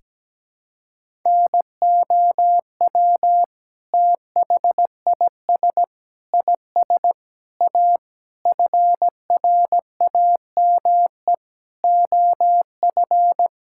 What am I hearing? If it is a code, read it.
Morse code, heard as a keyed tone, about 17 wpm, 700 Hz. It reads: NOW THIS IS A FRAME OF
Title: Sentences from Top 1000 Words - Review Part 6 - 17wpm